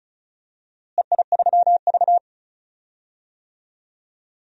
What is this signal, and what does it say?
Morse code, heard as a keyed tone, about 36 wpm, 700 Hz. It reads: EI3V
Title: DXCC-100 Call Signs - 4 Characters - Repeated 2x - 35wpm